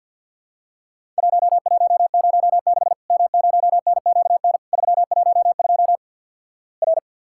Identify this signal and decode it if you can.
Morse code, heard as a keyed tone, about 48 wpm, 700 Hz. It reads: 1106 M0NQN 312 R
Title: Worked All Europe DX Contest - QTC Exchange - 50wpm